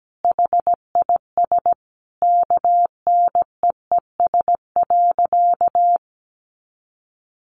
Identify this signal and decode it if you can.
Morse code, heard as a keyed tone, about 17 wpm, 700 Hz. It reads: HIS KNEES.
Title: A Sound Of Thunder 17wpm